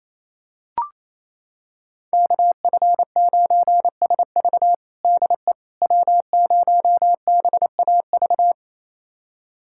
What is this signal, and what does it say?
Morse code, heard as a keyed tone, about 28 wpm, 700 Hz. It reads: KF9SV DE W0BAV
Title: USA Callsign DE Callsign - 5 Characters - 28wpm